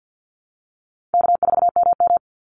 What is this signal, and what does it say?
Morse code, heard as a keyed tone, about 48 wpm, 700 Hz. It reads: K4MM
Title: Easy Call Signs Repeated 2x - 50wpm